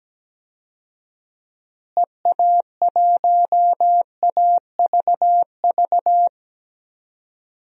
Morse code, heard as a keyed tone, about 17 wpm, 700 Hz. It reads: EA1AVV
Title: DXCC-100 Call Signs - 6 Characters - Repeated 3x - 17wpm